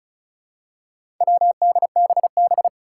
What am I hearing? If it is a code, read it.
Morse code, heard as a keyed tone, about 36 wpm, 700 Hz. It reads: WDBB
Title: Four Challenging Letter-Number Combinations 35wpm